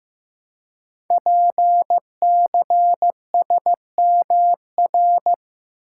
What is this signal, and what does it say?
Morse code, heard as a keyed tone, about 15 wpm, 700 Hz. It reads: PCSMR
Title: Code Groups - 5 Letter:Numbers - 15wpm